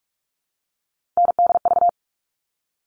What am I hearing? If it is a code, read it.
Morse code, heard as a keyed tone, about 44 wpm, 700 Hz. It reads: NDV
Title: All Cut Numbers - 3-Digit 45wpm